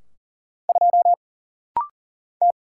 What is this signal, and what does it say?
Morse code, heard as a keyed tone, about 40 wpm, 700 Hz. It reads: 2 T
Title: Single Letter-Number Alt - Rapid-Fire 40wpm